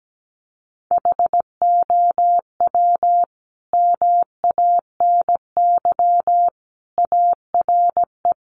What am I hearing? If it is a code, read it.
Morse code, heard as a keyed tone, about 17 wpm, 700 Hz. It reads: HOW MANY ARE
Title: Sentences from Top 300 Words 17wpm